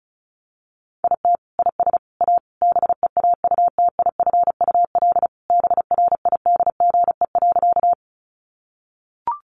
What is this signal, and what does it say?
Morse code, heard as a keyed tone, about 34 wpm, 700 Hz. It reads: IT IS A BEAUTIFUL BRIDGE.
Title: Sentences from Top 1200 Words 35wpm